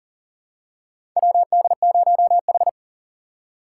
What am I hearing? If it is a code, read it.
Morse code, heard as a keyed tone, about 40 wpm, 700 Hz. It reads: WD0H
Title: USA Call Signs - 4 Characters - 4x Character Spacing - 40wpm